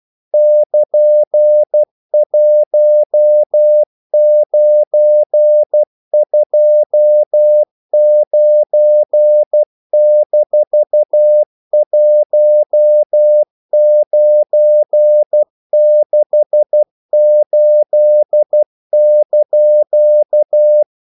Morse code, heard as a keyed tone, about 12 wpm, 600 Hz. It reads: <KN>1929-1968)
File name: CW / Practice / EnglishQuotations0112